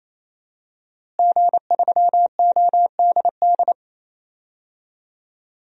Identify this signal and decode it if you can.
Morse code, heard as a keyed tone, about 27 wpm, 700 Hz. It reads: G3ODD
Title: DXCC-100 Call Signs - 5 Characters - Repeated 3x - 28wpm